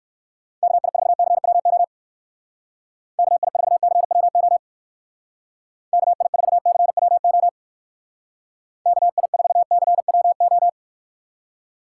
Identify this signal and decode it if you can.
Morse code, heard as a keyed tone, about 59 wpm, 700 Hz. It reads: KI4CWO KI4CWO KI4CWO KI4CWO